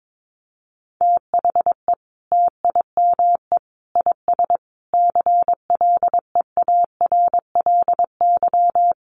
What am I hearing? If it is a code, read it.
Morse code, heard as a keyed tone, about 22 wpm, 700 Hz. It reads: THE TIME IS CLEARLY
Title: Sentences from Top 1300 Words 22wpm